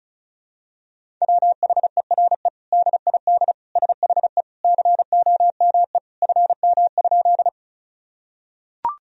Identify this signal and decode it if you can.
Morse code, heard as a keyed tone, about 36 wpm, 700 Hz. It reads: WHERE DID SHE COME FM?
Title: Sentences from Top 200 Words and Abbreviations 35wpm